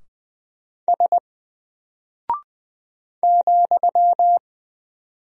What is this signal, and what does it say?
Morse code, heard as a keyed tone, about 20 wpm, 700 Hz. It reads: S ,
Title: Single Character (Letter:Number:Punctuation:Prosign) 20wpm